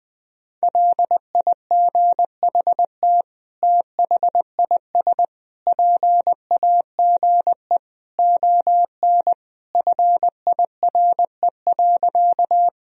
Morse code, heard as a keyed tone, about 20 wpm, 700 Hz. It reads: LIGHT THIS PAGE ON FIRE.
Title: Sentences from Top 1000 Words - Review Part 1 - 20wpm